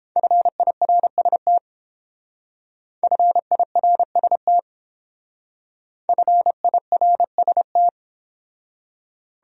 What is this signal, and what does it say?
Morse code, heard as a keyed tone, about 29 wpm, 700 Hz. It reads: FIRST FIRST FIRST